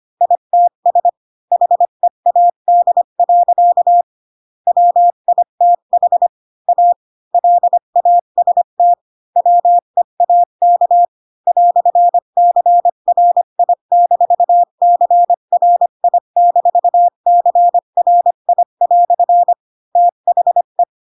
Morse code, heard as a keyed tone, about 25 wpm, 700 Hz. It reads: ITS HEAD. WITH A LAST WEAK "CRI-CRI-CRI" THE
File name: Pinocchio / Pinocchio ch4 20 25 ik1hge